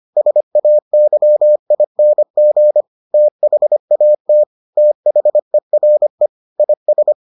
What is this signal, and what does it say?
Morse code, heard as a keyed tone, about 25 wpm, 600 Hz. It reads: SAYING THAT THERE IS